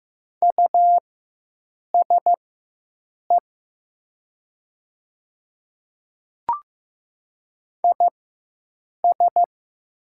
Morse code, heard as a keyed tone, about 15 wpm, spaced as for 7 wpm, 700 Hz. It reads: USE IS
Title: 4x Character Spacing - Top 100 Words - 15wpm